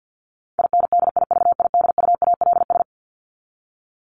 Morse code, heard as a keyed tone, about 50 wpm, 700 Hz. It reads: INDIVIDUALS
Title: ES:S Suffix - Top 100 Words with Base Words 50wpm